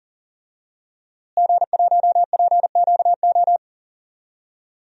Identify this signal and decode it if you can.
Morse code, heard as a keyed tone, about 37 wpm, 700 Hz. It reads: G1PQO
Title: DXCC-100 Call Signs - 5 Characters - Repeated 3x - 40wpm